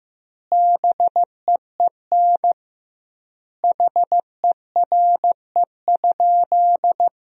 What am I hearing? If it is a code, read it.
Morse code, heard as a keyed tone, about 15 wpm, 700 Hz. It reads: BEEN HERE?
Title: 2x Word-Spacing - Sentences from Top 200 Words - 15wpm